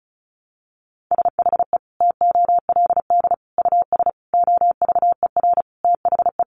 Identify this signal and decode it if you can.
Morse code, heard as a keyed tone, about 34 wpm, 700 Hz. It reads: SHE TOLD US OVER THE
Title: Sentences from Top 1100 Words 35wpm